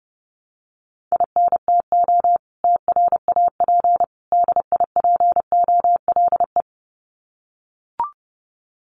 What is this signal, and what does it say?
Morse code, heard as a keyed tone, about 30 wpm, 700 Hz. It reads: INTO TRAP DIPOLE
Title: Common QSO Phrases 30wpm